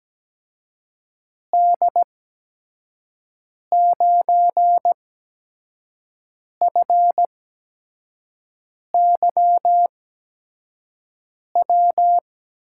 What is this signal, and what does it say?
Morse code, heard as a keyed tone, about 17 wpm, spaced as for 5 wpm, 700 Hz. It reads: D9FY W
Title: USA Call Signs - 8x Character Spacing - 17wpm